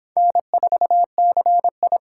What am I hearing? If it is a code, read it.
Morse code, heard as a keyed tone, about 25 wpm, 700 Hz. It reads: N4CI